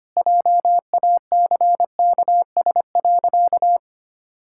Morse code, heard as a keyed tone, about 25 wpm, 700 Hz. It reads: JACKS.
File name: Easy Sentences with Hints - 25wpm